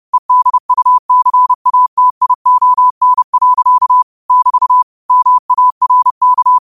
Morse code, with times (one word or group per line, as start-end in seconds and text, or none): 0.1-4.0 EDUCATION.
4.3-4.8 <BT>
5.1-6.6 MARK